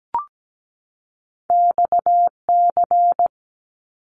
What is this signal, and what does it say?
Morse code, heard as a keyed tone, about 17 wpm, 700 Hz. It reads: XC
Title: Top Two Letter Combinations in Words 17wpm